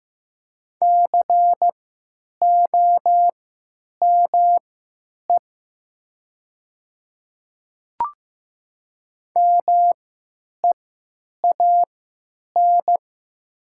Morse code, heard as a keyed tone, about 15 wpm, spaced as for 9 wpm, 700 Hz. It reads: COME MEAN
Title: 3x Character Spacing - Top 200 Words - 15wpm